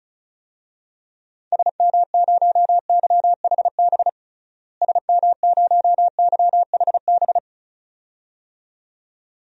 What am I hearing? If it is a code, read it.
Morse code, heard as a keyed tone, about 34 wpm, 700 Hz. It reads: SM0YHB SM0YHB
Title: DXCC-100 Call Signs - 6 Characters - Repeated 2x - 35wpm